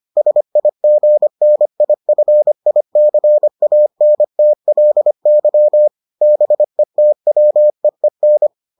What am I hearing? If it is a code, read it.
Morse code, heard as a keyed tone, about 25 wpm, 600 Hz. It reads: SIGNIFICANTLY BETWEEN